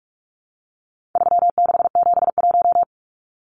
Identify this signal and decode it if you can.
Morse code, heard as a keyed tone, about 44 wpm, 700 Hz. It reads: 3671
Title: Numbers - 4-Digit 45wpm